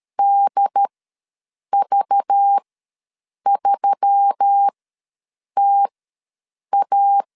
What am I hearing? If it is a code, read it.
Morse code, heard as a keyed tone, about 13 wpm, spaced as for 7 wpm, 800 Hz. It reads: DV3TA
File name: LEZIONI / 5 LBVC / 5 LBV ws 12wpm 3015